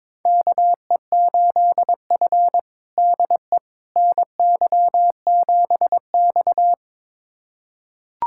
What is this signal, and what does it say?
Morse code, heard as a keyed tone, about 22 wpm, 700 Hz. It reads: KE8F DE NY7X
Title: USA Callsign DE Callsign - 4 Characters - 22wpm